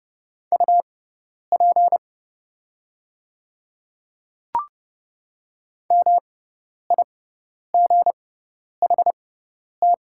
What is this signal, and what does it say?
Morse code, heard as a keyed tone, about 29 wpm, spaced as for 10 wpm, 700 Hz. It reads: UP MIGHT
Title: 6x Character Spacing - Top 300 Words - 30wpm